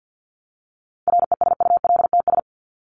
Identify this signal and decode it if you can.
Morse code, heard as a keyed tone, about 50 wpm, 700 Hz. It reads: RESULTS
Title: ES:S Suffix - Top 100 Words with Base Words 50wpm